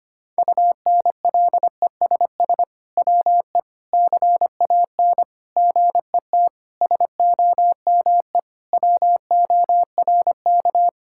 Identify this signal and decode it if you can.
Morse code, heard as a keyed tone, about 25 wpm, 700 Hz. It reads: UNLESS WE CAN GET SOME WORK
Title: Sentences from Top 1000 Words - Review Part 6 - 25wpm